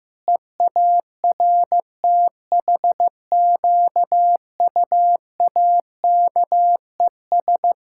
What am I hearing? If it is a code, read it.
Morse code, heard as a keyed tone, about 15 wpm, 700 Hz. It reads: EARTHQUAKES
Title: Pangram Challenge v2 - 15wpm